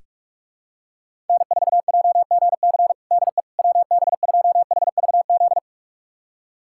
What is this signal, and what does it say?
Morse code, heard as a keyed tone, about 42 wpm, 700 Hz. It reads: N4JGC DE WB2HVZ
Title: USA Callsign DE Callsign 45wpm